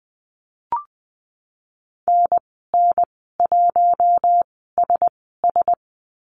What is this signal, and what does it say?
Morse code, heard as a keyed tone, about 20 wpm, spaced as for 15 wpm, 700 Hz. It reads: NN1SS
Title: 048 Lesson 9g - Number 4 - Call Signs 20wpm